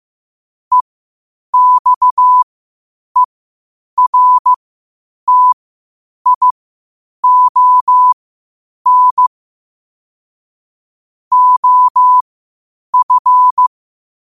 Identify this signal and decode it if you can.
Morse code, heard as a keyed tone, about 15 wpm, spaced as for 8 wpm, 1000 Hz. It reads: EXERTION OF
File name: qotdfiles / 8wpm